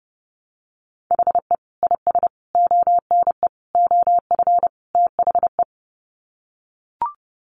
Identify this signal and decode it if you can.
Morse code, heard as a keyed tone, about 30 wpm, 700 Hz. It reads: HE IS ONE OF THE
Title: Sets of 5 Words Encore 30wpm